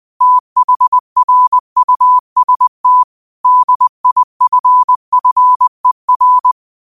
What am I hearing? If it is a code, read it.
Morse code, heard as a keyed tone, about 20 wpm, 1000 Hz. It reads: THRUST DIFFER